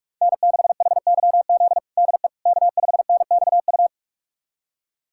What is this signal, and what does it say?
Morse code, heard as a keyed tone, about 43 wpm, 700 Hz. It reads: N6HYZ DE K5NXU
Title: USA Callsign DE Callsign - 5 Characters - 45wpm